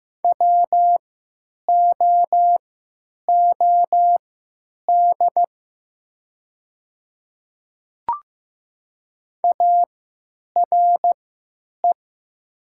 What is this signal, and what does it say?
Morse code, heard as a keyed tone, about 15 wpm, spaced as for 9 wpm, 700 Hz. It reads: WOOD ARE
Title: 3x Character Spacing - Top 300 Words - 15wpm